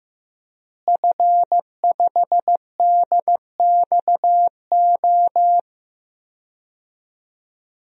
Morse code, heard as a keyed tone, about 15 wpm, 700 Hz. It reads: F5DXO
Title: DXCC-100 Call Signs - 5 Characters - Repeated 3x - 15wpm